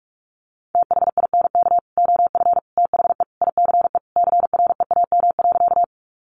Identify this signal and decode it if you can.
Morse code, heard as a keyed tone, about 44 wpm, 700 Hz. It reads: THINK OF THE ICE CREAM.
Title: Sentences from Top 1500 Words 45wpm